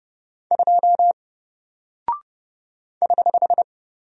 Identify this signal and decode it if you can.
Morse code, heard as a keyed tone, about 30 wpm, 700 Hz. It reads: 2 <HH>
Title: Single Character (Letter:Number:Punctuation:Prosign) 30wpm